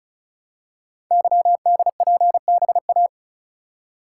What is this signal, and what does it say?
Morse code, heard as a keyed tone, about 36 wpm, 700 Hz. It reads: YDPBA
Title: Code Groups - 5 Letters - 35wpm